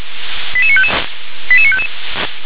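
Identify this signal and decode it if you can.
Five-tone selective call (ZVEI), heard as quick runs of five tones, tones of about 70 ms. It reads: 88955 88955
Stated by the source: Five-tone ZVEI selective call received on SDR.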